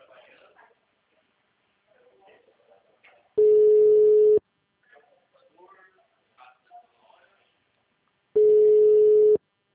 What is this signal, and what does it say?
Ringback tone (Europe, 425 Hz), with speech.
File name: out- -2021- - - .23450